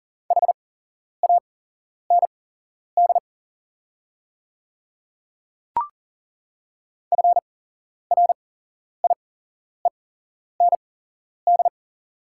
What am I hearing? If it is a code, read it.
Morse code, heard as a keyed tone, about 40 wpm, spaced as for 11 wpm, 690 Hz. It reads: HAND FRIEND
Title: 8x Character Spacing - Top 500 Words - 40wpm